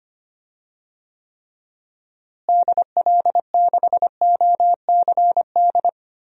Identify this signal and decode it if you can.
Morse code, heard as a keyed tone, about 25 wpm, 700 Hz. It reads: DL6OCD
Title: DXCC-100 Call Signs - 6 Characters - Repeated 3x - 25wpm